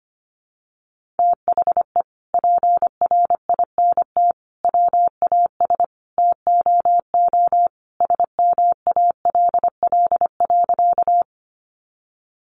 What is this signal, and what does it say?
Morse code, heard as a keyed tone, about 25 wpm, 700 Hz. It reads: THE PRINT WAS TOO SMALL.
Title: Sentences from Top 1400 Words 25wpm